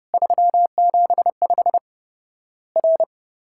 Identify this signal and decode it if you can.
Morse code, heard as a keyed tone, about 30 wpm, 700 Hz. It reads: 375 R
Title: Worked All Europe DX Contest - QTC Exchange - 30wpm